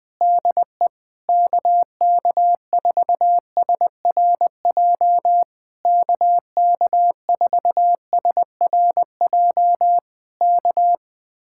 Morse code, heard as a keyed tone, about 21 wpm, 700 Hz. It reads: DE KK4SRJ KK4SRJ K
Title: Speed-Racing Common QSO Phrases Encore - 20wpm